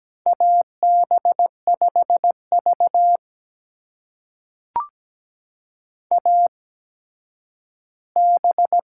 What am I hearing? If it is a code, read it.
Morse code, heard as a keyed tone, about 17 wpm, spaced as for 10 wpm, 700 Hz. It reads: AB5V AB
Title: USA Call Signs - 4 Characters - 8x Character Spacing - 17wpm